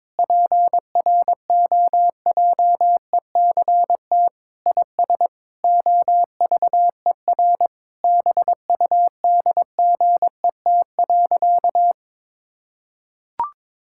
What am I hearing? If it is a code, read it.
Morse code, heard as a keyed tone, about 22 wpm, 700 Hz. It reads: PROJECT IS OVER BUDGET.
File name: practice / 22wpm / Sentences with 8-Word Phrases - 22wpm